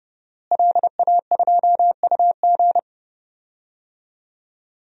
Morse code, heard as a keyed tone, about 30 wpm, 700 Hz. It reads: LA2UG
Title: DXCC-100 Call Signs - 5 Characters - Repeated 2x - 30wpm